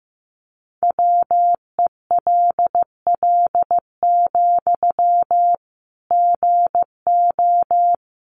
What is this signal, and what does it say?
Morse code, heard as a keyed tone, about 14 wpm, 700 Hz. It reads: WELL, GO
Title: Sentences from Top 1100 Words 15wpm